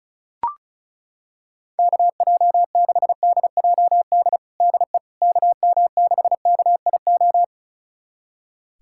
Morse code, heard as a keyed tone, about 36 wpm, 700 Hz. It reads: KJ6DJD DE KM6KIO
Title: USA Callsign DE Callsign 35wpm